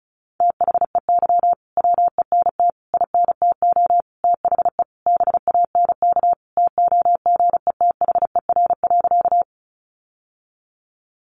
Morse code, heard as a keyed tone, about 34 wpm, 700 Hz. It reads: THEY WENT INTO THE BANK TOGETHER.
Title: Sentences from Top 900 Words 35wpm